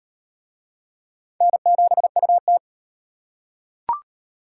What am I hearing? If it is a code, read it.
Morse code, heard as a keyed tone, about 38 wpm, 700 Hz. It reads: N7UT